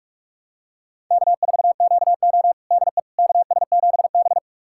Morse code, heard as a keyed tone, about 46 wpm, 700 Hz. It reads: K4QO DE KS7B
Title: USA Callsign DE Callsign - 4 Characters - 45wpm